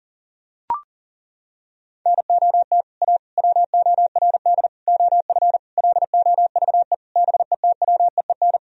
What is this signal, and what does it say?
Morse code, heard as a keyed tone, about 37 wpm, 700 Hz. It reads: NOT A WORD OF LOVE BETWEEN
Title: Sentences from Top 1000 Words - Review Part 1 - 40wpm